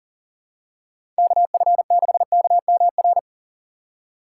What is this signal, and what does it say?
Morse code, heard as a keyed tone, about 39 wpm, 700 Hz. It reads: KF6KMR
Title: USA Call Signs - 6 Characters - 4x Character Spacing - 40wpm